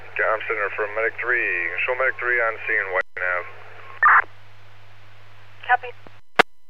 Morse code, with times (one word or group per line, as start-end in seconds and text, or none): none